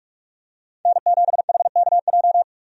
Morse code, heard as a keyed tone, about 41 wpm, 700 Hz. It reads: N7HKJ
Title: USA Call Signs - 5 Characters - Repeated 3x - 45wpm